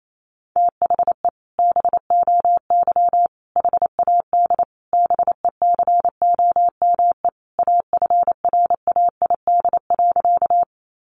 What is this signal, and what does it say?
Morse code, heard as a keyed tone, about 28 wpm, 700 Hz. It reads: THE BOY HAD BECOME AFRAID.
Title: Sentences from Top 1500 Words 28wpm